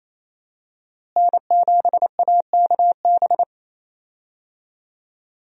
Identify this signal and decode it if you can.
Morse code, heard as a keyed tone, about 28 wpm, 700 Hz. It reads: N7AKB
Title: USA Call Signs - 5 Characters - Repeated 2x - 28wpm